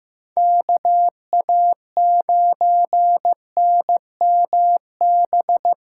Morse code, heard as a keyed tone, about 15 wpm, 700 Hz. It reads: KA9NMB
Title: USA Callsign DE Callsign 15wpm